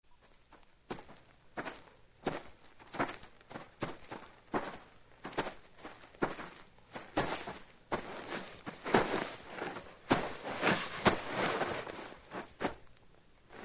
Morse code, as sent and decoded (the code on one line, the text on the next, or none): none